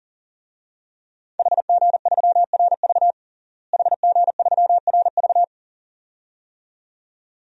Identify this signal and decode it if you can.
Morse code, heard as a keyed tone, about 40 wpm, 700 Hz. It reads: HG3RV HG3RV
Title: DXCC-100 Call Signs - 5 Characters - Repeated 2x - 40wpm